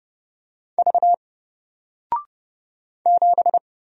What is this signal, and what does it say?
Morse code, heard as a keyed tone, about 30 wpm, 700 Hz. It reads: V 7
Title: Single Letter-Number Alt 30wpm